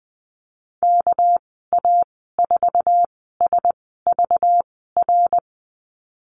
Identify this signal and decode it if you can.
Morse code, heard as a keyed tone, about 20 wpm, spaced as for 14 wpm, 700 Hz. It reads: KA4SVR
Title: 207 Lesson 33g - Letter j - Call Signs 20wpm